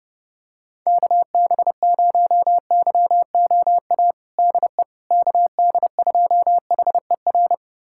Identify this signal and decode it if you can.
Morse code, heard as a keyed tone, about 32 wpm, 700 Hz. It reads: KB0YOA DE KD2HER
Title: USA Callsign DE Callsign - 6 Characters - 30wpm